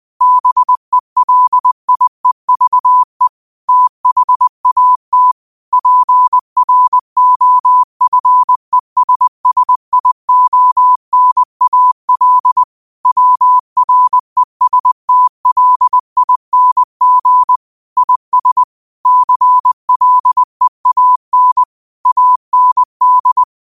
Morse code, heard as a keyed tone, about 20 wpm, 1000 Hz. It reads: BELIEVE THAT PROFESSIONAL WRESTLING IS CLEAN AND
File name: qotdfiles / 20wpm